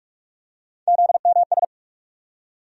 Morse code, heard as a keyed tone, about 46 wpm, 700 Hz. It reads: ZMS
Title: USA Call Sign Postfixes - 3 Letters 45wpm